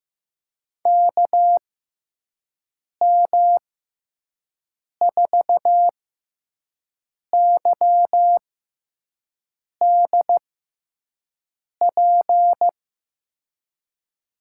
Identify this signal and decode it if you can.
Morse code, heard as a keyed tone, about 15 wpm, spaced as for 5 wpm, 700 Hz. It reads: KM4YDP
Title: USA Call Signs - 6 Characters - 6x Character Spacing - 15wpm